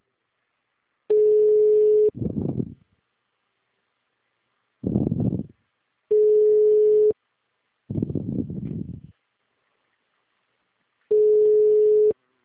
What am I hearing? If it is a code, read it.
Ringback tone (Europe, 425 Hz).